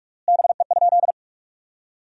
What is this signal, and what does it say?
Morse code, heard as a keyed tone, about 46 wpm, 700 Hz. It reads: BE?
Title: 3x Word-Spacing - Sentences from Top 100 Words - 45wpm